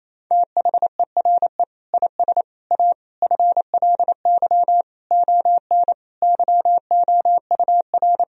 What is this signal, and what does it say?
Morse code, heard as a keyed tone, about 27 wpm, 700 Hz. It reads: THERE IS A FLY ON YOUR
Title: Sentences from Top 1000 Words - Review Part 2 - 28wpm